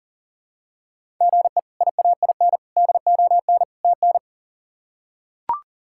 Morse code, heard as a keyed tone, about 40 wpm, 700 Hz. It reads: GE IAIN DON TN
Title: K1USN SST QSO Practice - 40wpm